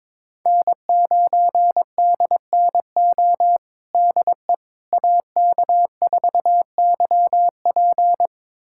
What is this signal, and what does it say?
Morse code, heard as a keyed tone, about 21 wpm, 700 Hz. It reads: N9DNO DE AK4YP
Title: USA Callsign DE Callsign - 5 Characters - 22wpm